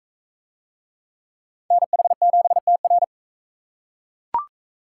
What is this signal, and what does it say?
Morse code, heard as a keyed tone, about 44 wpm, 700 Hz. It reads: NH7TR